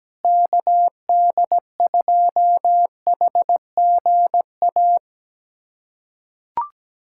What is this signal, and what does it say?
Morse code, heard as a keyed tone, about 17 wpm, 700 Hz. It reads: KD2HGA
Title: USA Callsign DE Callsign 17wpm